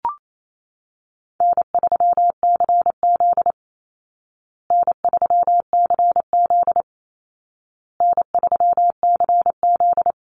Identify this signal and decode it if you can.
Morse code, heard as a keyed tone, about 28 wpm, 700 Hz. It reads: N3CZ N3CZ N3CZ
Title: Call Signs Repeated 2x - 28wpm